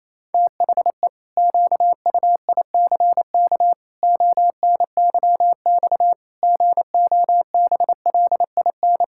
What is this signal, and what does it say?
Morse code, heard as a keyed tone, about 28 wpm, 700 Hz. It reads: THE QUICK ONYX GOBLIN
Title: Pangram Challenge v2 - 28wpm